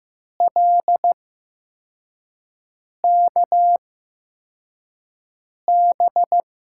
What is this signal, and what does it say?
Morse code, heard as a keyed tone, about 15 wpm, spaced as for 4 wpm, 700 Hz. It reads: LKB
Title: USA Call Signs - 6 Characters - 8x Character Spacing - 15wpm